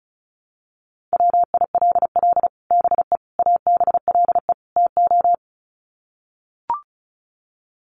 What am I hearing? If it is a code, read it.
Morse code, heard as a keyed tone, about 34 wpm, 700 Hz. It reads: WILL BE ABLE TO
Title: Sets of 4 Words 35wpm